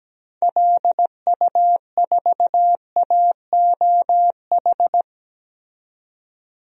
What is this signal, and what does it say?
Morse code, heard as a keyed tone, about 17 wpm, 700 Hz. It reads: LU4AOH
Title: DXCC-100 Call Signs - 6 Characters - 17wpm